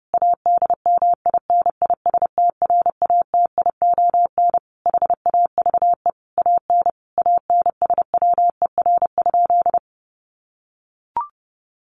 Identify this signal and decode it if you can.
Morse code, heard as a keyed tone, about 30 wpm, 700 Hz. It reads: ADMINISTRATION HAVE AN ANSWER?
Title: Sentences from Top 1300 Words 30wpm